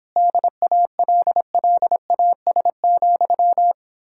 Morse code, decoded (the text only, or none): DALLAS,